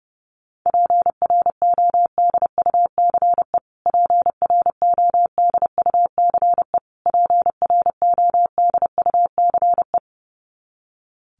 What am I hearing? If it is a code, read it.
Morse code, heard as a keyed tone, about 28 wpm, 700 Hz. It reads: PRODUCE PRODUCE PRODUCE
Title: Top 500 Words Repeated 30wpm